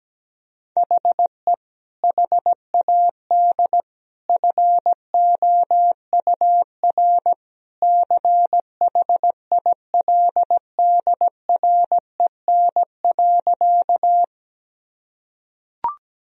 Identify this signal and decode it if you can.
Morse code, heard as a keyed tone, about 17 wpm, 700 Hz. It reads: HE HAD FOUR CHILDREN.
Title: Sentences from Top 1000 Words - Review Part 1 - 17wpm